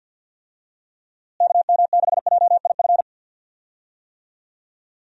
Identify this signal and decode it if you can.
Morse code, heard as a keyed tone, about 45 wpm, 700 Hz. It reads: KM6JIF
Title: USA Call Signs - 6 Characters - Repeated 2x - 50wpm